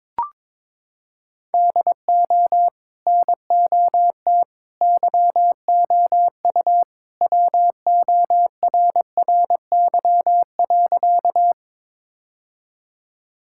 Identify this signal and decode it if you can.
Morse code, heard as a keyed tone, about 22 wpm, 700 Hz. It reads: DO NOT YOU WORRY.
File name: practice / 22wpm / Sentences with 3-Word Phrases - 22wpm